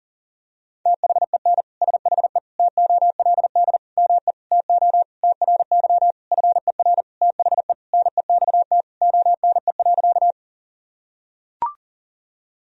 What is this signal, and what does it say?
Morse code, heard as a keyed tone, about 40 wpm, 700 Hz. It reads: THEN SHE TOLD ME TO TRY FER THE NEXT ONE.
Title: Sentences from Top 400 Words and Abbreviations 40wpm